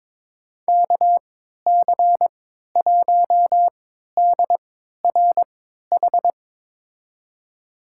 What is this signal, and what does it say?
Morse code, heard as a keyed tone, about 21 wpm, spaced as for 12 wpm, 700 Hz. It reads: KC1DRH
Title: USA Call Signs - 6 Characters - 3x Character Spacing - 22wpm